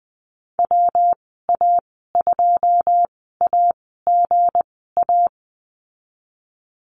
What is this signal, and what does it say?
Morse code, heard as a keyed tone, about 19 wpm, spaced as for 14 wpm, 700 Hz. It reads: WA2AGA
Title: 231 Lesson 37g - Letter x - Call Signs 20wpm